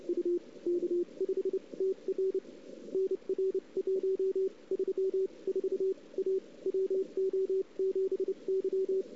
Morse code, heard as a keyed tone, about 30 wpm, spaced as for 25 wpm, 410 Hz. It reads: U K5TR NR 134 A WO7Y
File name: CW / 2008-ss-cw-k5tr-op-n5rz -02:00